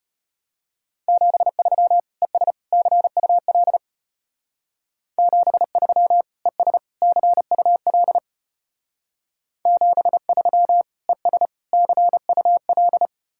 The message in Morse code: 73 ES CUL 73 ES CUL 73 ES CUL

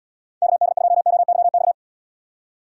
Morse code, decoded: KD2OYX

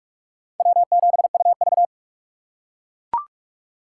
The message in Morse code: W7UV

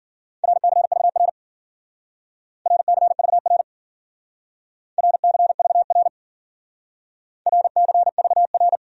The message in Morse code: RCVR RCVR RCVR RCVR

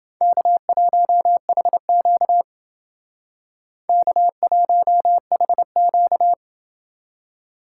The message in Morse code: K1HQ K1HQ

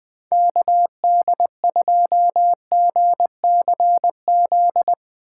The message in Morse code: KD2GCZ

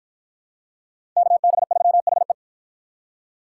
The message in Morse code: KB3HE

